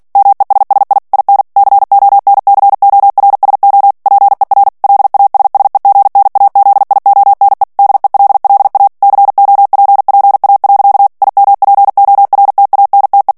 Morse code, heard as a keyed tone, about 48 wpm, 800 Hz. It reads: MESSI IN CONCORSO PER LASSEGNAZIONE DELLA COPPA. IMPORTANTE
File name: ITALIEN texte clair 2 vitesse 45 mots minute